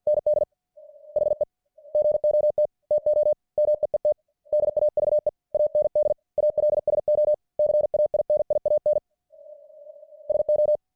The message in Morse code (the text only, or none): ND HE GOT TO MEET DAVE AND ALSO CAINIAN SO